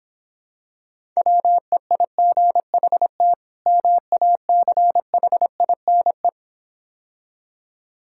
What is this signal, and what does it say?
Morse code, heard as a keyed tone, about 26 wpm, 700 Hz. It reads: WEIGHT MACHINE